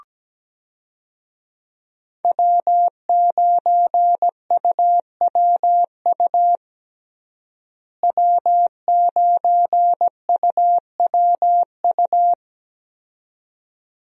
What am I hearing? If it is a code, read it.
Morse code, heard as a keyed tone, about 17 wpm, 700 Hz. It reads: W9UWU W9UWU
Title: USA Call Signs - 5 Characters - Repeated 2x - 17wpm